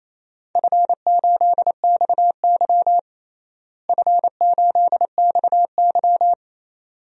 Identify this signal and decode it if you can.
Morse code, heard as a keyed tone, about 28 wpm, 700 Hz. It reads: F8XY F8XY
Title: DXCC-100 Call Signs - 4 Characters - Repeated 2x - 28wpm